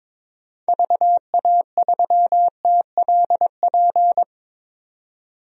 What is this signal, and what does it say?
Morse code, heard as a keyed tone, about 22 wpm, 700 Hz. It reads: VA3TLP
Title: DXCC-100 Call Signs - 6 Characters - 22wpm